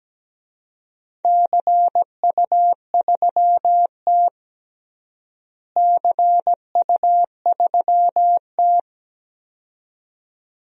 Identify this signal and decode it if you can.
Morse code, heard as a keyed tone, about 17 wpm, 700 Hz. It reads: CU3T CU3T
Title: DXCC-100 Call Signs - 4 Characters - Repeated 2x - 17wpm